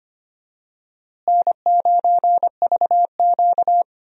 Morse code, heard as a keyed tone, about 25 wpm, 700 Hz. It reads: N9VQ
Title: USA Call Signs - 4 Characters - Repeated 3x - 25wpm